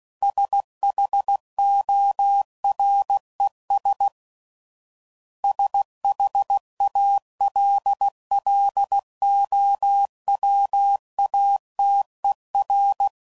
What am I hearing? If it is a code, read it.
Morse code, heard as a keyed tone, about 16 wpm, 800 Hz. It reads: SHORES SHALLOWATER
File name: CW Code Groups @15WPM / fromSM's / Cities